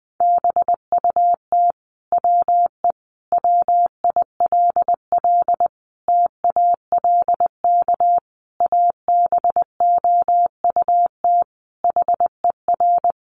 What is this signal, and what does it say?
Morse code, heard as a keyed tone, about 19 wpm, 700 Hz. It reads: BUT WE WILL TALK ABOUT HER
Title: Sentences from Top 1200 Words 20wpm